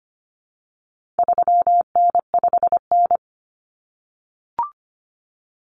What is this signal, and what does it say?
Morse code, heard as a keyed tone, about 25 wpm, 700 Hz. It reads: 3N5N
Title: Common Cut Numbers - 4-Digit 25wpm